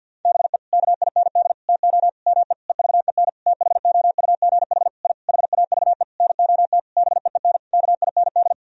Morse code, heard as a keyed tone, about 46 wpm, 700 Hz. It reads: BE KIND TO ME EVEN THOUGH I HAVE NOT BEEN KIND
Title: Sentences from Top 1000 Words - Review Part 2 - 50wpm